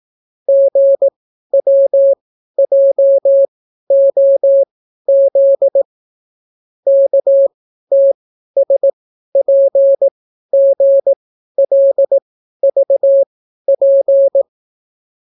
Morse code, heard as a keyed tone, about 18 wpm, spaced as for 12 wpm, 550 Hz. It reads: GWJOZ KTSPGLVP